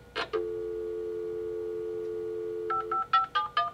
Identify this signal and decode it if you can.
Dial tone (North America), 350 + 440 Hz.
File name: scanner / libsounds.com - Fax